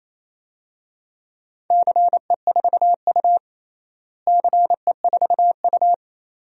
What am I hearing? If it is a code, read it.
Morse code, heard as a keyed tone, about 27 wpm, 700 Hz. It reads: CE4U CE4U
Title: DXCC-100 Call Signs - 4 Characters - Repeated 2x - 28wpm